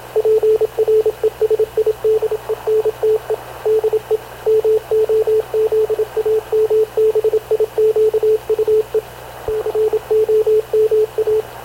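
Morse code, heard as a keyed tone, about 27 wpm, 430 Hz, with radio noise. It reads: PRESIDENTE DE MOZAMBIQUE COMA
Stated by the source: Havana Radio CLA press (PX) broadcast, recorded by Radio Officer Paul Dumesnil aboard mv CNAV QUEST/CZDO.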